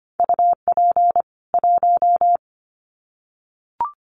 Morse code, heard as a keyed tone, about 25 wpm, 700 Hz. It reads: UP 1
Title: Common QSO Phrases 25wpm